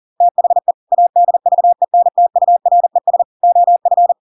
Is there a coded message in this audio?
Yes: Morse code from 0.2 to 4.1 s, heard as a keyed tone, about 40 wpm, 700 Hz. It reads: THE ADVENTURES OF